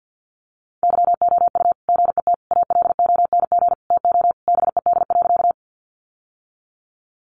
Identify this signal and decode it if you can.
Morse code, heard as a keyed tone, about 48 wpm, 700 Hz. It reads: YOU GET ALONG TO BED.
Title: Sentences from Top 700 Words 50wpm